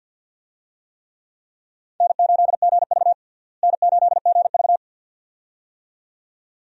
Morse code, heard as a keyed tone, about 52 wpm, 700 Hz. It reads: N8GV N8GV
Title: USA Call Signs - 4 Characters - Repeated 2x - 50wpm